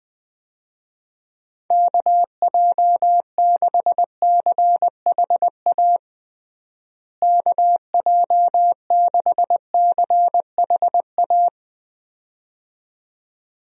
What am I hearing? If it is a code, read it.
Morse code, heard as a keyed tone, about 20 wpm, 700 Hz. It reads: KJ6CHA KJ6CHA
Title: USA Call Signs - 6 Characters - Repeated 2x - 20wpm